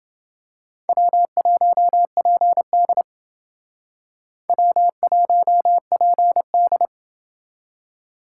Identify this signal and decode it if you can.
Morse code, heard as a keyed tone, about 29 wpm, 700 Hz. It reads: W1PD W1PD